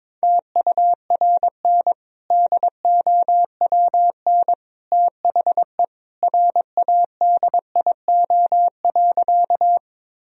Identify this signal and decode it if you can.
Morse code, heard as a keyed tone, about 22 wpm, 700 Hz. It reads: TURN DOWN THE RADIO.
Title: Sentences from Top 1000 Words - Review Part 3 - 22wpm